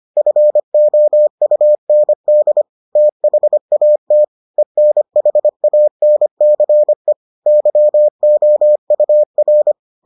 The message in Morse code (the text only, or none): FOUND THAT ENHANCE YOUR